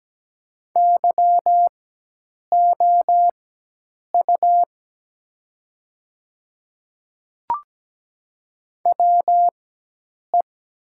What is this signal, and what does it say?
Morse code, heard as a keyed tone, about 17 wpm, spaced as for 8 wpm, 700 Hz. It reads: YOU WE
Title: 4x Character Spacing - Top 200 Words - 17wpm